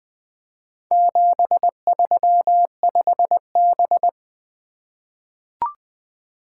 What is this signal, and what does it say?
Morse code, heard as a keyed tone, about 20 wpm, 700 Hz. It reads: 735B
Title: Four Letter-Number Combinations - 20wpm